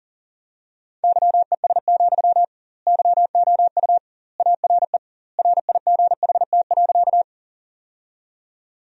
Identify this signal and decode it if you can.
Morse code, heard as a keyed tone, about 40 wpm, 700 Hz. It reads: YES, YOU ARE RIGHT.
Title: 2x Word-Spacing - Sentences from Top 500 Words - 40wpm